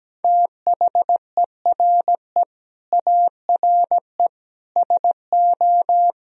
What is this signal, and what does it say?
Morse code, heard as a keyed tone, about 17 wpm, 700 Hz. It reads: THERE ARE SO
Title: Speed-Racing - Sets of 4 Words 17wpm